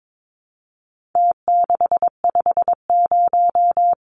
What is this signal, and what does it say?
Morse code, heard as a keyed tone, about 22 wpm, 700 Hz. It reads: T650
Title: Numbers - Contest Style - 4-Digit 22wpm